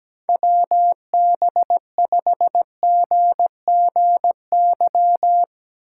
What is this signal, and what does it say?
Morse code, heard as a keyed tone, about 17 wpm, 700 Hz. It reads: WB5GGY
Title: Speed Racing - USA 2x3 Call Signs 17wpm